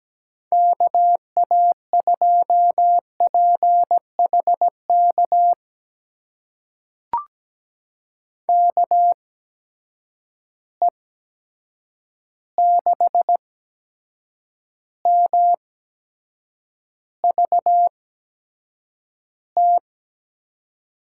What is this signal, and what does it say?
Morse code, heard as a keyed tone, about 17 wpm, spaced as for 7 wpm, 700 Hz. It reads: KA2PHK KE6MVT
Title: USA Call Signs - 6 Characters - 8x Character Spacing - 17wpm